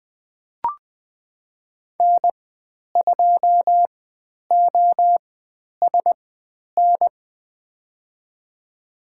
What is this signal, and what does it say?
Morse code, heard as a keyed tone, about 20 wpm, spaced as for 10 wpm, 700 Hz. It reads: N2OSN
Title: 095 Lesson 16g - Letter c - Call Signs 20wpm